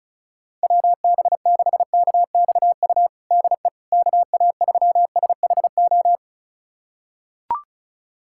Morse code, heard as a keyed tone, about 36 wpm, 700 Hz. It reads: WB6KXU DE KA3SHO
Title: USA Callsign DE Callsign 35wpm